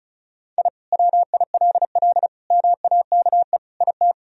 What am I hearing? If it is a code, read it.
Morse code, heard as a keyed tone, about 34 wpm, 700 Hz. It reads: I WILL MAKE IT
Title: Sentences from Top 1000 Words - Review Part 2 - 35wpm